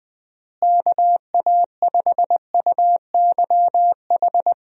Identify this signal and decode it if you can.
Morse code, heard as a keyed tone, about 20 wpm, 700 Hz. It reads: KA5UYH
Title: USA Call Signs - 6 Characters - Repeated 3x - 20wpm